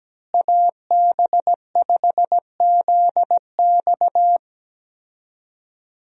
Morse code, heard as a keyed tone, about 17 wpm, 700 Hz. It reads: AB5ZX
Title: USA Call Signs - 5 Characters - Repeated 3x - 17wpm